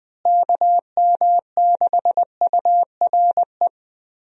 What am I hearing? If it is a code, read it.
Morse code, heard as a keyed tone, about 20 wpm, 700 Hz. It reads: KM6URE
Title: USA Call Signs - 6 Characters - Repeated 3x - 20wpm